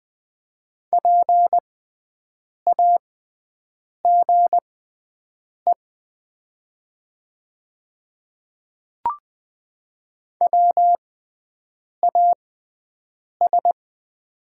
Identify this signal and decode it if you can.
Morse code, heard as a keyed tone, about 20 wpm, spaced as for 7 wpm, 700 Hz. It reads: PAGE WAS
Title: 6x Character Spacing - Top 300 Words - 20wpm